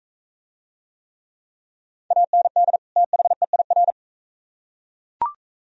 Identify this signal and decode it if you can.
Morse code, heard as a keyed tone, about 42 wpm, 700 Hz. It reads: AND THEIR